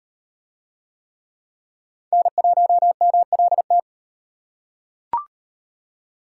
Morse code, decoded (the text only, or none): N1MLT